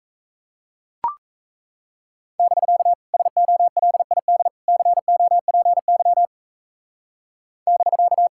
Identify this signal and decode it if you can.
Morse code, heard as a keyed tone, about 35 wpm, 700 Hz. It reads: <BK> SOLID COPY <BK>
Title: Speed-Racing Common QSO Phrases - 28wpm